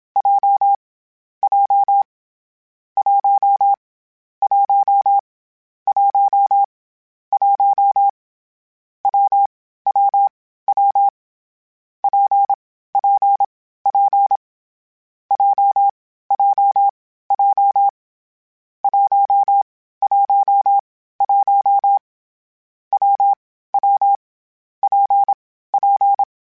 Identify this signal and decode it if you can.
Morse code, heard as a keyed tone, about 27 wpm, 800 Hz. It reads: J J 1 1 1 1 W W W P P P J J J 1 1 1 W W P P
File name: Difficult Combos / Difficult Combos-mp3s / WPJ WPJ1 @25